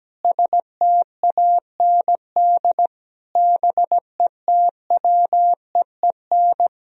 Morse code, heard as a keyed tone, about 17 wpm, 700 Hz. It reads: STAND BETWEEN
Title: Sentences from Top 1000 Words - Review Part 1 - 17wpm